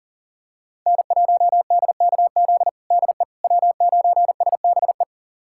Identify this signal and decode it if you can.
Morse code, heard as a keyed tone, about 40 wpm, 700 Hz. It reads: N1DKZ DE W9SBE
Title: USA Callsign DE Callsign - 5 Characters - 40wpm